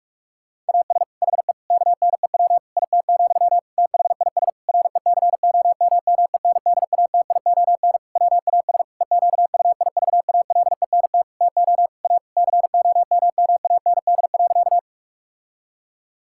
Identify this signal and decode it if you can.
Morse code, heard as a keyed tone, about 41 wpm, 700 Hz. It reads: AS HE KNEW IT, THIS RECOMMENDATION WAS EQUIVALENT TO A COMMAND.
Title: Sentences from Top 1600 Words 45wpm